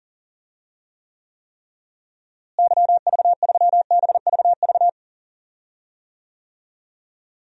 Morse code, heard as a keyed tone, about 40 wpm, 700 Hz. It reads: YV3BVV
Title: DXCC-100 Call Signs - 6 Characters - Repeated 3x - 40wpm